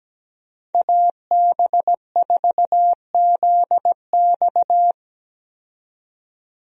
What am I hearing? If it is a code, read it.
Morse code, heard as a keyed tone, about 17 wpm, 700 Hz. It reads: AB4ZX
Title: USA Call Signs - 5 Characters - 17wpm